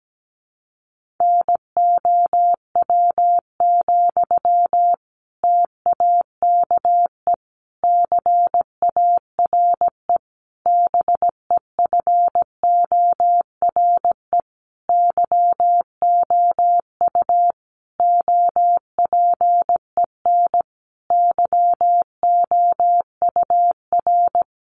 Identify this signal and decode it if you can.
Morse code, heard as a keyed tone, about 17 wpm, 700 Hz. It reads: NOW, TAKE CARE BEFORE YOU OPEN YOUR
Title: Sentences from Top 1500 Words 17wpm